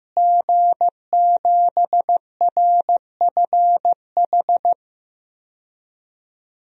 Morse code, heard as a keyed tone, about 15 wpm, 700 Hz. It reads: G7RFH
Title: DXCC-100 Call Signs - 5 Characters - Repeated 2x - 15wpm